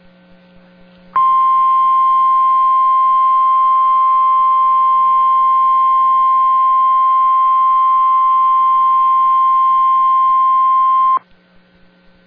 Emergency Alert System (SAME) tones (the 1050 Hz attention tone). It is heard from 1.1 to 11.2 s.